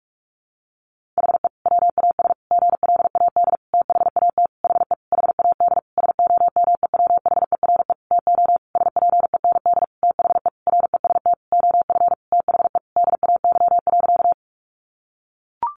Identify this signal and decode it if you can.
Morse code, heard as a keyed tone, about 44 wpm, 700 Hz. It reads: HE WAS GLAD THAT HE HAD SOMEWHERE TO SPEND THE REST OF THE DAY.
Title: Sentences from Top 1400 Words 45wpm